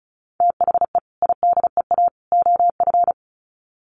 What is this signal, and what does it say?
Morse code, heard as a keyed tone, about 34 wpm, 700 Hz. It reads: THE IDEA OF
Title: Sets of 3 Words 35wpm